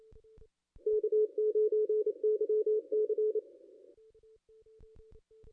Morse code, heard as a keyed tone, about 33 wpm, 410 Hz. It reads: K9YC
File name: contests / ss-cw-k5tr-op-n5rz -0600